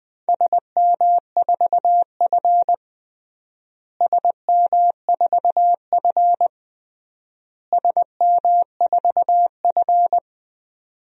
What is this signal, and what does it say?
Morse code, heard as a keyed tone, about 20 wpm, 700 Hz. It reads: SM4F SM4F SM4F
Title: DXCC-100 Call Signs - 4 Characters - Repeated 3x - 20wpm